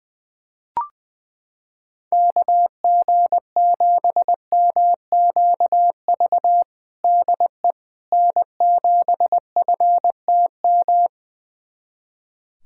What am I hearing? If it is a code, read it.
Morse code, heard as a keyed tone, about 20 wpm, 700 Hz. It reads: KG7MQV DE N7FTM
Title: USA Callsign DE Callsign 20wpm